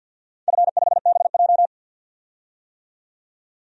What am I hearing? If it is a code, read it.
Morse code, heard as a keyed tone, about 49 wpm, 700 Hz. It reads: F5BJ
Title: DXCC-100 Call Signs - 4 Characters - 50wpm